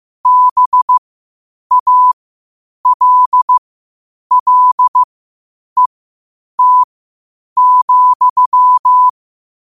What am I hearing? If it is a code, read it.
Morse code, heard as a keyed tone, about 15 wpm, spaced as for 8 wpm, 1000 Hz. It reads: BALLET,
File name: qotdfiles / 8wpm